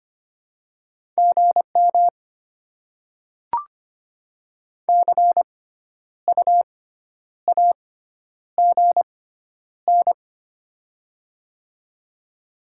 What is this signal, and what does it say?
Morse code, heard as a keyed tone, about 25 wpm, spaced as for 10 wpm, 700 Hz. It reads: GM CU AGN
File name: Top 20 QSO Elements - 6x Character Spacing - 25wpm